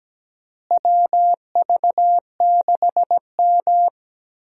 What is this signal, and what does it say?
Morse code, heard as a keyed tone, about 17 wpm, 700 Hz. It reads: WV6M
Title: USA Call Signs - 4 Characters - Repeated 3x - 17wpm